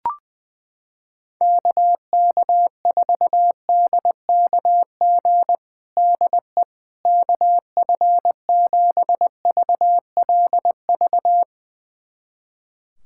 Morse code, heard as a keyed tone, about 21 wpm, 700 Hz. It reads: KK4DKG DE KF7VLV
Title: USA Callsign DE Callsign 20wpm